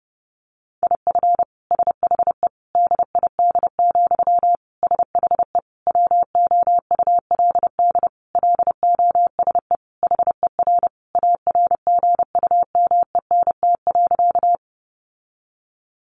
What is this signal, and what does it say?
Morse code, heard as a keyed tone, about 30 wpm, 700 Hz. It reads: IF SHE DID, SHE WOULD LOSE HER ARGUMENT.
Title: Sentences from Top 1400 Words 30wpm